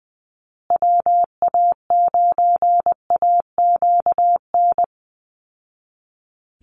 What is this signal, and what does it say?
Morse code, heard as a keyed tone, about 20 wpm, 700 Hz. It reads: WA9AQN
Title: Call Signs Repeated 2x - 20wpm